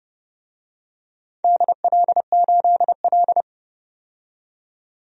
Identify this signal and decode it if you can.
Morse code, heard as a keyed tone, about 30 wpm, 700 Hz. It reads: DL8L
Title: DXCC-100 Call Signs - 4 Characters - 30wpm